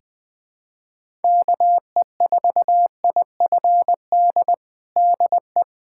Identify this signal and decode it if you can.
Morse code, heard as a keyed tone, about 21 wpm, 700 Hz. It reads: KE4IFD DE
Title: USA Callsign DE Callsign - 6 Characters - 20wpm